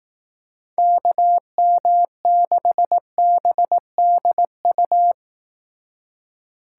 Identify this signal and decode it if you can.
Morse code, heard as a keyed tone, about 18 wpm, 700 Hz. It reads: KM6BDU
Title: Speed Racing - USA 2x3 Call Signs 15wpm